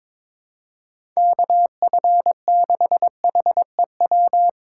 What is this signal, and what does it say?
Morse code, heard as a keyed tone, about 21 wpm, 700 Hz. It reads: KF6HEW
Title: USA Call Signs - 3x Character Spacing - 22wpm